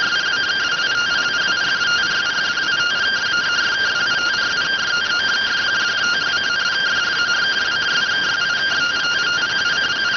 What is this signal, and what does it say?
Two-tone teleprinter signal, about 170 Hz shift.